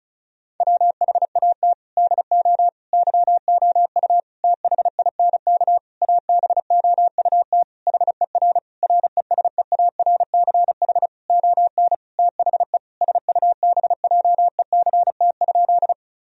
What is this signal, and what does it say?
Morse code, heard as a keyed tone, about 36 wpm, 700 Hz. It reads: WHAT DO YOU THINK ABOUT HER RESEARCH ON THE SUBJECT?
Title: Sentences from Top 1000 Words - Review Part 4 - 35wpm